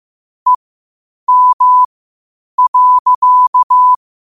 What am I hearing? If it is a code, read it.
Morse code, heard as a keyed tone, about 15 wpm, spaced as for 8 wpm, 1000 Hz. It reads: EM.
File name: qotdfiles / 8wpm